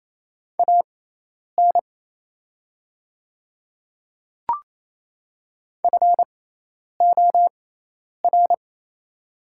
Morse code, heard as a keyed tone, about 28 wpm, spaced as for 10 wpm, 700 Hz. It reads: AN FOR
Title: 6x Character Spacing - Top 100 Words - 28wpm